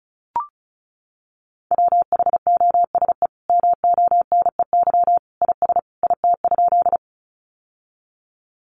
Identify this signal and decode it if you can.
Morse code, heard as a keyed tone, about 34 wpm, 700 Hz. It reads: WHOSE MONEY IS IT?
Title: Sentences from Top 1200 Words 35wpm